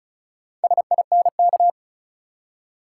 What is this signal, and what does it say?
Morse code, heard as a keyed tone, about 34 wpm, 700 Hz. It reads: SINK
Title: Four Letter Words - 35wpm